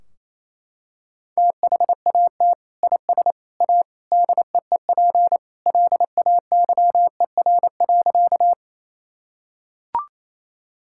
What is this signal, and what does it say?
Morse code, heard as a keyed tone, about 28 wpm, 700 Hz. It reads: THAT IS A DEEP LAYER.
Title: Sentences from Top 1600 Words 28wpm